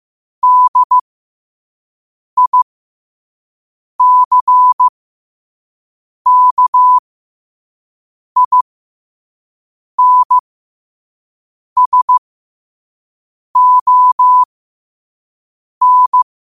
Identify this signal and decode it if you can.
Morse code, heard as a keyed tone, about 15 wpm, spaced as for 5 wpm, 1000 Hz. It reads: DICKINSON
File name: qotdfiles / 5wpm